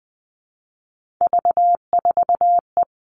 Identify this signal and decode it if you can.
Morse code, heard as a keyed tone, about 19 wpm, 700 Hz. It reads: V4E
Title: Three Challenging Letter-Number Combinations 20wpm